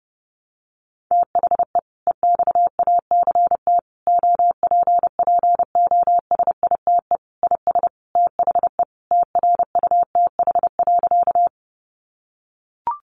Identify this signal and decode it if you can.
Morse code, heard as a keyed tone, about 30 wpm, 700 Hz. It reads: THE EXACT OPPOSITE IS THE TRUTH.
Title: Sentences from Top 1300 Words 30wpm